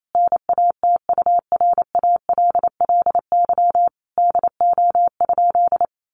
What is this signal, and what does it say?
Morse code, heard as a keyed tone, about 28 wpm, 700 Hz. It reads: NATURALLY DO?
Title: Sentences from Top 1300 Words 28wpm